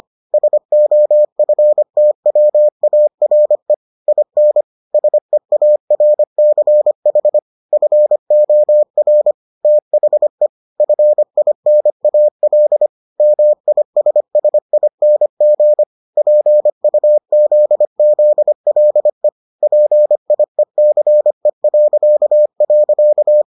Morse code, heard as a keyed tone, about 25 wpm, 600 Hz. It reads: SOFTWARE IN SEARCH FOR THE FINAL MISSING PUZZLE PIECE..